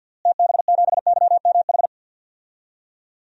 Morse code, heard as a keyed tone, about 51 wpm, 700 Hz. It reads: TB7YMH
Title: DXCC-100 Call Signs - 6 Characters - Repeated 3x - 50wpm